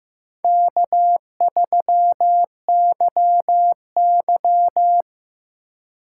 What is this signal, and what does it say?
Morse code, heard as a keyed tone, about 15 wpm, 700 Hz. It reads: K3YY
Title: USA Call Signs - 4 Characters - 4x Character Spacing - 15wpm